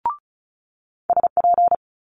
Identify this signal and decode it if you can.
Morse code, heard as a keyed tone, about 36 wpm, 700 Hz. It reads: SP
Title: Multi-letter Phonemes Quick Fire 35wpm